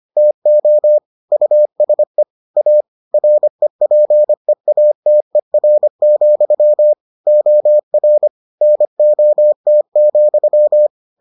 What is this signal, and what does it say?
Morse code, heard as a keyed tone, about 25 wpm, 600 Hz. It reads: TO USE A REPEATER, OR NOT,